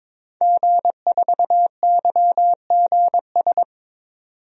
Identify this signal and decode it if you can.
Morse code, heard as a keyed tone, about 22 wpm, 700 Hz. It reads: G4YGS
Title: DXCC-100 Call Signs - 5 Characters - 22wpm